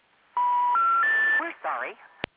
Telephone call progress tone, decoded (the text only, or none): special information tone